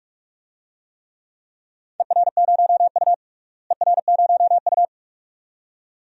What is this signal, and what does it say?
Morse code, heard as a keyed tone, about 46 wpm, 700 Hz. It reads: ER0U ER0U
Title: DXCC-100 Call Signs - 4 Characters - Repeated 2x - 45wpm